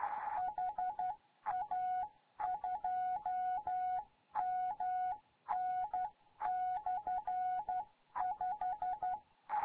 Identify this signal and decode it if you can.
Morse code, heard as a keyed tone, about 12 wpm, 740 Hz. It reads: HA2MN/5
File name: GYAKORLAT